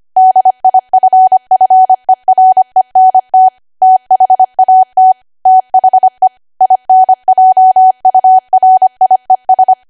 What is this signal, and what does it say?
Morse code, heard as a keyed tone, about 25 wpm, 750 Hz. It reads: DIFFERENT THAT THE INJURIES
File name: cwa / W1AW-Level3 / 25WPM